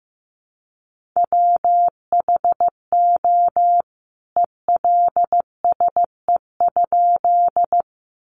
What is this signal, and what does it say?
Morse code, heard as a keyed tone, about 15 wpm, 700 Hz. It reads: WHO ELSE?
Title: A Sound Of Thunder 15wpm